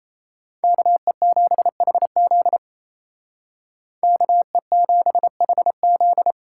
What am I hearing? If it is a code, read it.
Morse code, heard as a keyed tone, about 34 wpm, 700 Hz. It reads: KE7HZ KE7HZ
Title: Speed Racing - USA 2x2 Call Signs 28wpm